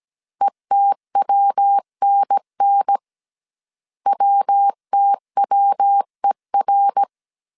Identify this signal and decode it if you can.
Morse code, heard as a keyed tone, about 17 wpm, 800 Hz. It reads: ETWNN WTWER